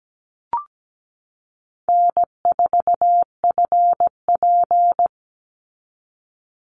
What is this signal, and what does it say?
Morse code, heard as a keyed tone, about 17 wpm, 700 Hz. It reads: N4FP
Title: Call Signs Repeated 2x - 17wpm